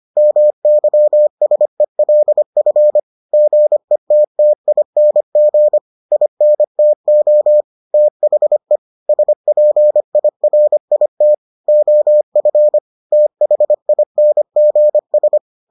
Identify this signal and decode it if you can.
Morse code, heard as a keyed tone, about 25 wpm, 600 Hz. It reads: MYSELF GETTING INTO THE SPIRIT OF THINGS